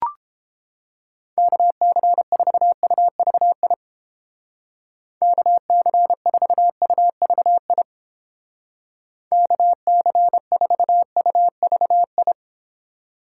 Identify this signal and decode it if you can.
Morse code, heard as a keyed tone, about 28 wpm, 700 Hz. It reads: KC4UVI KC4UVI KC4UVI